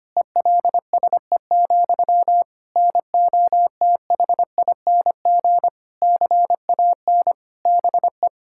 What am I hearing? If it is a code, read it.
Morse code, heard as a keyed tone, about 25 wpm, 700 Hz. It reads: ELSE, NOTHING CAN BE
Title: Sentences from Top 1600 Words 25wpm